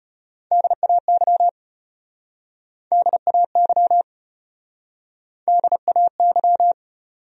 Morse code, heard as a keyed tone, about 33 wpm, 700 Hz. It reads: DAY DAY DAY